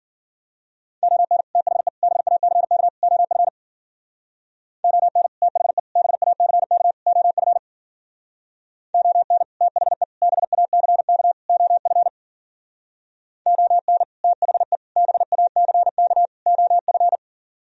Morse code, heard as a keyed tone, about 51 wpm, 700 Hz. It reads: ON THE BACK OF ON THE BACK OF ON THE BACK OF ON THE BACK OF